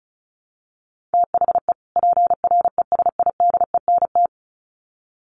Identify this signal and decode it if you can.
Morse code, heard as a keyed tone, about 34 wpm, 700 Hz. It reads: THE PRESIDENT
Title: Sets of 2 Words 35wpm